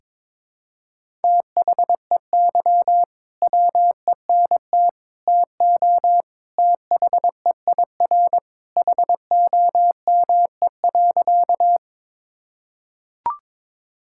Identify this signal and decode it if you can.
Morse code, heard as a keyed tone, about 22 wpm, 700 Hz. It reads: THEY WENT TO THEIR HOME.
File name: practice / 22wpm / Sentences with 2-Word Phrases - 22wpm